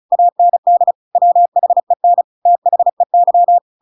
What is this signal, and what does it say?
Morse code, heard as a keyed tone, about 34 wpm, 700 Hz. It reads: AND WHEN THEY